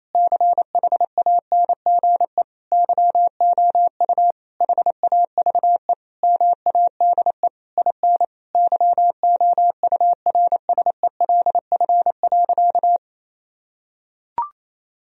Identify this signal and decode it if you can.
Morse code, heard as a keyed tone, about 28 wpm, 700 Hz. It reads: CHANGE YOU HAVE MADE IN YOURSELF.
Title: Sentences from Top 1000 Words - Review Part 1 - 28wpm